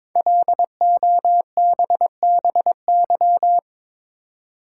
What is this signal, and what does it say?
Morse code, heard as a keyed tone, about 22 wpm, 700 Hz. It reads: LOBBY
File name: 5 Letter Words - First 3 Letters Given - 22wpm